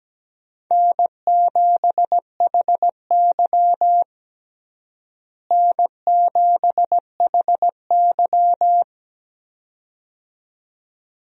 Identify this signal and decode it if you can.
Morse code, heard as a keyed tone, about 17 wpm, 700 Hz. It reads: N7HY N7HY
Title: USA Call Signs - 4 Characters - Repeated 2x - 17wpm